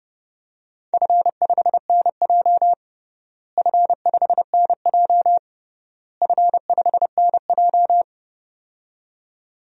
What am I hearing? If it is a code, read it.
Morse code, heard as a keyed tone, about 30 wpm, 700 Hz. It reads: F5NJ F5NJ F5NJ
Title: DXCC-100 Call Signs - 4 Characters - Repeated 3x - 30wpm